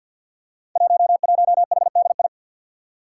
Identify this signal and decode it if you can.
Morse code, heard as a keyed tone, about 51 wpm, 700 Hz. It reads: 11HDI
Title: Five Challenging Letter-Number Combinations 50wpm